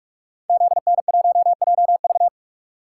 Morse code, heard as a keyed tone, about 44 wpm, 700 Hz. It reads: ZN1JV
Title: Code Groups - 5 Letter:Numbers - 45wpm